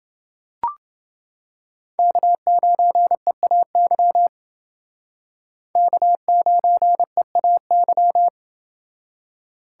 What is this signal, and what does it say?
Morse code, heard as a keyed tone, about 28 wpm, 700 Hz. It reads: K9EAY K9EAY